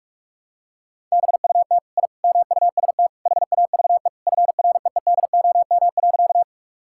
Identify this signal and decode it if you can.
Morse code, heard as a keyed tone, about 46 wpm, 700 Hz. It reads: BUT I MUST HAVE FREEDOM.
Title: Sentences from Top 1000 Words - Review Part 5 - 45wpm